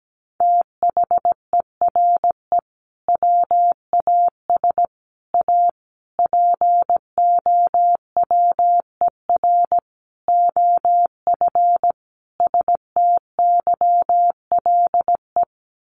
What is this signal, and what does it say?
Morse code, heard as a keyed tone, about 17 wpm, 700 Hz. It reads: THERE WAS A POWER OF STYLE
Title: Sentences from Top 800 Words 17wpm